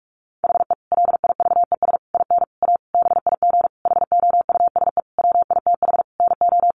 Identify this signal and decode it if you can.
Morse code, heard as a keyed tone, about 44 wpm, 700 Hz. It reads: HE LIVES IN A BIG HOUSE WITH NO
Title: Additional Easy Phrases 45wpm